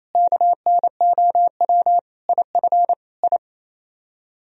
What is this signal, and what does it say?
Morse code, heard as a keyed tone, about 38 wpm, spaced as for 33 wpm, 700 Hz. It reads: KNOW IF I
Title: Speed-Racing - Sets of 5 Words 28wpm